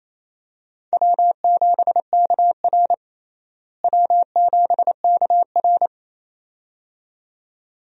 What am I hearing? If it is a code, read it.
Morse code, heard as a keyed tone, about 28 wpm, 700 Hz. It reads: W7KR W7KR
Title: USA Call Signs - 4 Characters - Repeated 2x - 28wpm